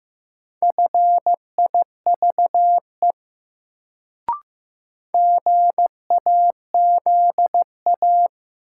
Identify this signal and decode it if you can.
Morse code, heard as a keyed tone, about 15 wpm, 700 Hz. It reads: FIVE GAZA
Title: Sets of 5 Words - New Word Drill - 15wpm